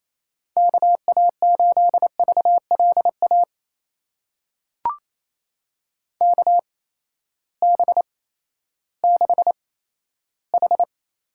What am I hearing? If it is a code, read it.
Morse code, heard as a keyed tone, about 28 wpm, spaced as for 14 wpm, 700 Hz. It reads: KA8VLA KB6H
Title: USA Call Signs - 6 Characters - 8x Character Spacing - 28wpm